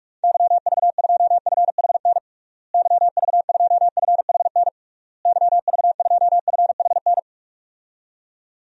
Morse code, heard as a keyed tone, about 46 wpm, 700 Hz. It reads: YV2FHN YV2FHN YV2FHN
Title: DXCC-100 Call Signs - 6 Characters - Repeated 3x - 45wpm